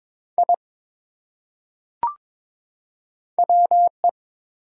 Morse code, heard as a keyed tone, about 22 wpm, 700 Hz. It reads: I WE
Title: Top 100 Words 22wpm